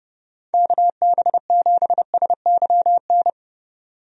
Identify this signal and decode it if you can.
Morse code, heard as a keyed tone, about 30 wpm, 700 Hz. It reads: KB7SYN
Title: USA Call Signs - 6 Characters - 30wpm